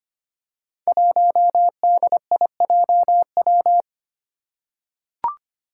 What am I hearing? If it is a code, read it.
Morse code, heard as a keyed tone, about 25 wpm, 700 Hz. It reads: 1DIJW
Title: Five Challenging Letter-Number Combinations 25wpm